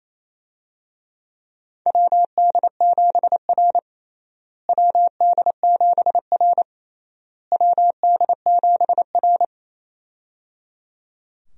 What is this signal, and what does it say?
Morse code, heard as a keyed tone, about 28 wpm, 700 Hz. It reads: WD7R WD7R WD7R
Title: USA Call Signs - 4 Characters - Repeated 3x - 28wpm